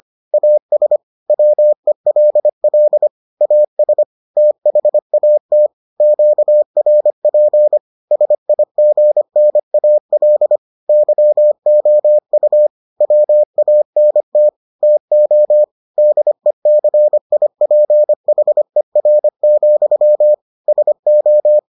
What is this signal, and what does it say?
Morse code, heard as a keyed tone, about 25 wpm, 600 Hz. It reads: AS WELL AS THAT QRP SIGNAL YOU WANT TO DECIPHER, SO